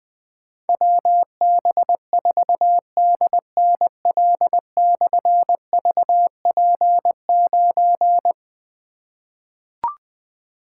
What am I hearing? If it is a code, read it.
Morse code, heard as a keyed tone, about 20 wpm, 700 Hz. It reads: WB4DNL/VP9
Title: Difficult Call Signs 20wpm